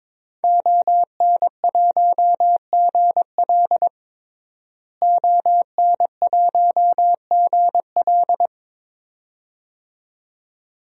Morse code, heard as a keyed tone, about 22 wpm, 700 Hz. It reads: ON1GL ON1GL
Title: DXCC-100 Call Signs - 5 Characters - Repeated 2x - 22wpm